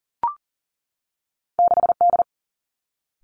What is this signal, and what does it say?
Morse code, heard as a keyed tone, about 40 wpm, 700 Hz. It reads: 6 D
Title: Two Challenging Letter-Number Combinations 40wpm